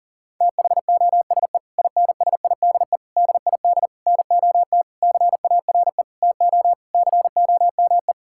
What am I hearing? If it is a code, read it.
Morse code, heard as a keyed tone, about 40 wpm, 700 Hz. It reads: THOSE INSIDE DID NOT CARE TO COME
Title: Sentences from Top 1000 Words - Review Part 6 - 40wpm